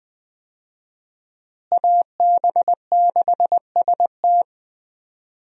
Morse code, heard as a keyed tone, about 21 wpm, 700 Hz. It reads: AB6ST
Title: USA Call Signs - 5 Characters - Repeated 3x - 20wpm